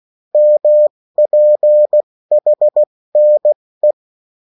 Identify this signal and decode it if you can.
Morse code, heard as a keyed tone, about 16 wpm, spaced as for 14 wpm, 600 Hz. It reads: MPHNE
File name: morse mp3 / revision9-vitesse 14 mots